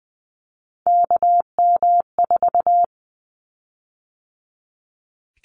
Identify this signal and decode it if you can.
Morse code, heard as a keyed tone, about 20 wpm, 700 Hz. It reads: KM4
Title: Callsign Prefixes for Top 25 DXCC - 20wpm